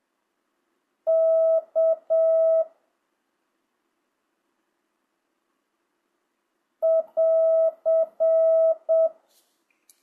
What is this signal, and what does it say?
Morse code, heard as a keyed tone, about 7 wpm, 1300 Hz. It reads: K <AR>
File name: jh8cbh / 2025cw